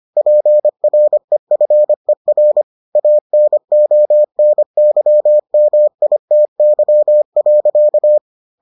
Morse code, heard as a keyed tone, about 25 wpm, 600 Hz. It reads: PREFER ANONYMITY.